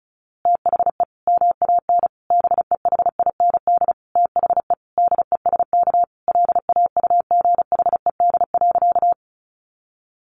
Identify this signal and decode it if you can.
Morse code, heard as a keyed tone, about 34 wpm, 700 Hz. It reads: THE MAN BEHIND THE DESK LAUGHED.
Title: A Sound Of Thunder 35wpm